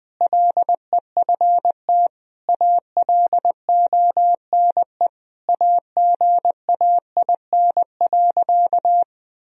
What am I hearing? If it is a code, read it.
Morse code, heard as a keyed tone, about 20 wpm, 700 Hz. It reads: LEFT ALONE AGAIN.
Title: Sentences from Top 1000 Words - Review Part 1 - 20wpm